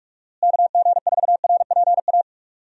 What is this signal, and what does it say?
Morse code, heard as a keyed tone, about 43 wpm, 700 Hz. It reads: KG4RPA
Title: USA Call Signs - 6 Characters - Repeated 3x - 45wpm